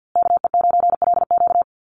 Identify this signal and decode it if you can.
Morse code, heard as a keyed tone, about 52 wpm, 700 Hz. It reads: KE9LQ
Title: USA Call Signs Repeated 3x 50wpm